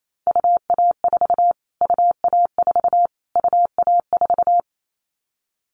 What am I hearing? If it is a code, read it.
Morse code, heard as a keyed tone, about 28 wpm, 700 Hz. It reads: UA4 UA4 UA4
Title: Callsign Prefixes for Top 25 DXCC - Repeated 3x - 28wpm